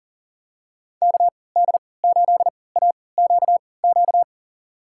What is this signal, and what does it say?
Morse code, heard as a keyed tone, about 40 wpm, spaced as for 23 wpm, 700 Hz. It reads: KD8AQQ
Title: USA Call Signs - 6 Characters - 3x Character Spacing - 40wpm